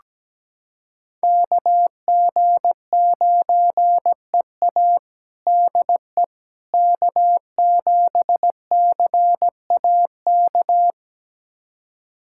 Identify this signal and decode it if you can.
Morse code, heard as a keyed tone, about 17 wpm, 700 Hz. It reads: KG9EA DE K7CAK
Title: USA Callsign DE Callsign - 5 Characters - 17wpm